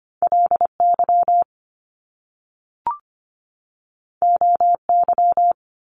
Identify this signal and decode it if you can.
Morse code, heard as a keyed tone, about 25 wpm, 700 Hz. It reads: LY OY
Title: Top Two Letter Combinations in Words 25wpm